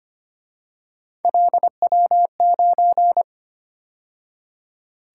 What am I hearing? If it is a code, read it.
Morse code, heard as a keyed tone, about 29 wpm, 700 Hz. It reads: LW9
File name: Speed-Racing - Three Letter-Number Combinations - 25wpm